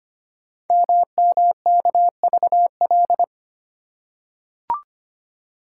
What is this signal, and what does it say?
Morse code, heard as a keyed tone, about 25 wpm, 700 Hz. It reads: MMKVL
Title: Code Groups - 5 Letters - 25wpm